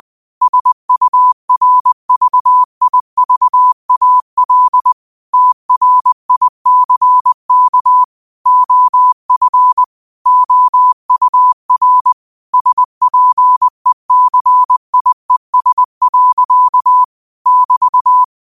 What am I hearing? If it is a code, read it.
Morse code, heard as a keyed tone, about 20 wpm, 1000 Hz. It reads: SURVIVAL TRICK OF OUR SPECIES. <BT>
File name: 20wpm